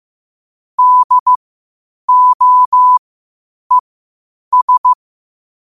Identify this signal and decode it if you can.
Morse code, heard as a keyed tone, about 15 wpm, spaced as for 8 wpm, 1000 Hz. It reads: DOES
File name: qotdfiles / 8wpm